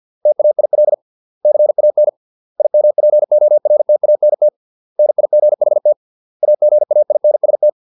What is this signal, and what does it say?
Morse code, heard as a keyed tone, about 50 wpm, 600 Hz. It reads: TAIL CAN IMPORTANT NIGHT AGAINST
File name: 632 words / 632 Words 50WPM